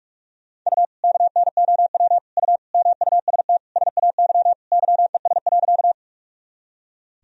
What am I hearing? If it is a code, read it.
Morse code, heard as a keyed tone, about 45 wpm, 700 Hz. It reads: U KNOW U MUST SAY YES.
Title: Sentences from Top 500 Words and Abbreviations 45wpm